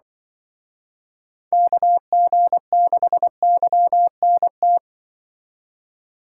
Morse code, heard as a keyed tone, about 25 wpm, 700 Hz. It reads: KG6YNT
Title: Speed Racing - USA Call Signs 20wpm